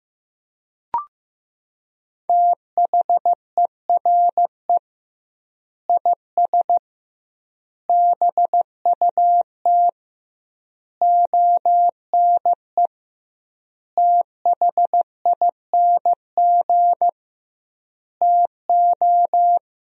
Morse code, heard as a keyed tone, about 15 wpm, 700 Hz. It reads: THERE IS BUT ONE THING TO
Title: 2x Word-Spacing - Sentences from Top 100 Words - 15wpm